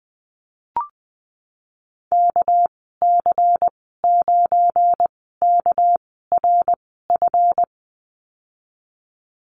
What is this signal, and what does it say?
Morse code, heard as a keyed tone, about 20 wpm, spaced as for 15 wpm, 700 Hz. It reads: KC9KRF
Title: 238 Lesson 38g - Letter q - Call Signs 20wpm